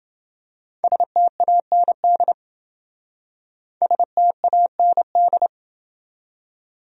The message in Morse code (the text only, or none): STAND STAND